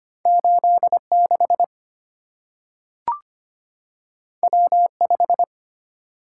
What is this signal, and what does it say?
Morse code, heard as a keyed tone, about 25 wpm, 700 Hz. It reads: 86 W5
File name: Two Letter-Number Combinations - 25wpm